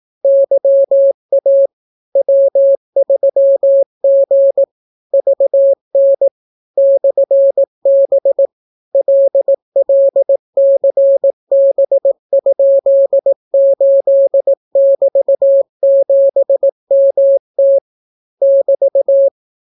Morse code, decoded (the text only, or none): YA W3G VN /B LLCB?8<BT>7MT <BT>